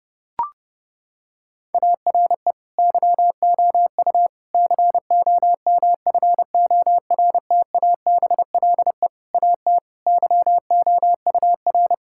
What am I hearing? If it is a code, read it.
Morse code, heard as a keyed tone, about 30 wpm, 700 Hz. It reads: ARE YOU COMFORTABLE AT YOUR
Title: Sentences from Top 1600 Words 30wpm